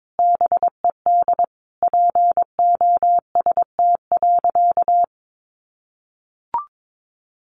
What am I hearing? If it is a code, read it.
Morse code, heard as a keyed tone, about 22 wpm, 700 Hz. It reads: BED POST.
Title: Sentences from Top 1000 Words 22wpm